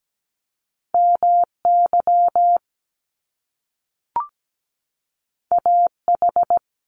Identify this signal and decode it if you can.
Morse code, heard as a keyed tone, about 17 wpm, 700 Hz. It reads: MY AH
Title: Top Two Letter Combinations in Words 17wpm